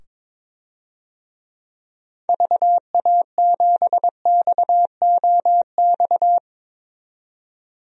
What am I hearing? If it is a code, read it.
Morse code, heard as a keyed tone, about 22 wpm, 700 Hz. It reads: VA7XOX
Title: DXCC-100 Call Signs - 6 Characters - 22wpm